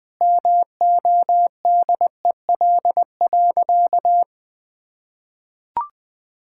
Morse code, heard as a keyed tone, about 21 wpm, 700 Hz. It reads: MODEL.
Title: Sentences with 7-Word Phrases - Spoken Beginning - 20wpm